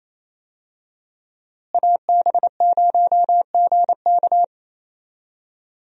Morse code, heard as a keyed tone, about 28 wpm, 700 Hz. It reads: AB0GK
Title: USA Call Signs - 5 Characters - Repeated 2x - 28wpm